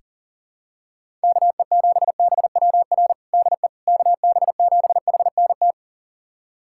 Morse code, heard as a keyed tone, about 40 wpm, 700 Hz. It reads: KE7BWR DE KB7HNT
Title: USA Callsign DE Callsign - 6 Characters - 40wpm